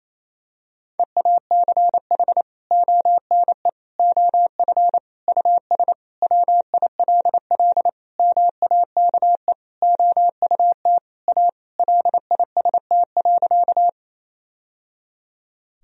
Morse code, heard as a keyed tone, about 27 wpm, 700 Hz. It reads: EACH ONE OF US WILL MAKE OUT A LIST.
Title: Sentences from Top 1000 Words - Review Part 2 - 28wpm